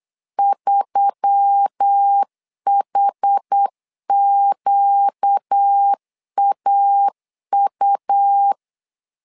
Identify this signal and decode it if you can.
Morse code, heard as a keyed tone, about 8 wpm, 800 Hz. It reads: 3HQAU
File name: LEZIONI / 5 LBVC / 5 LB 8wpm 2816